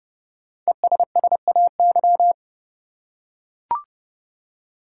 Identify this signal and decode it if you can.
Morse code, heard as a keyed tone, about 31 wpm, 700 Hz. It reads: ESSAY
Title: Five Letter Words - 30wpm